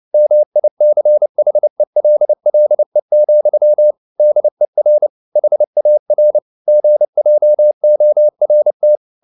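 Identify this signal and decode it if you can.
Morse code, heard as a keyed tone, about 29 wpm, 600 Hz. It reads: MICHELLE, DER HAR GJORT
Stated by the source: Fast (QRQ) Morse at 29 wpm, from CW med Gnister.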